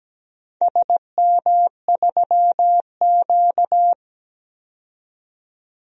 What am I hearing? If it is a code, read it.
Morse code, heard as a keyed tone, about 17 wpm, 700 Hz. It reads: SM3Q
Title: DXCC-100 Call Signs - 4 Characters - 17wpm